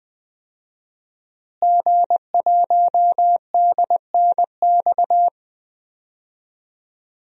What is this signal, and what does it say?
Morse code, heard as a keyed tone, about 20 wpm, 700 Hz. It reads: G1DNX
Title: DXCC-100 Call Signs - 5 Characters - Repeated 3x - 20wpm